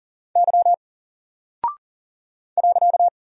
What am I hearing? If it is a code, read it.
Morse code, heard as a keyed tone, about 38 wpm, 700 Hz. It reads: Y .
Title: Single Character (Letter:Number:Punctuation:Prosign) 40wpm